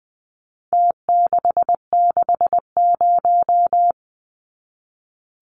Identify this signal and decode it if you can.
Morse code, heard as a keyed tone, about 20 wpm, 700 Hz. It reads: T660
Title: Numbers - Contest Style - 4-Digit 20wpm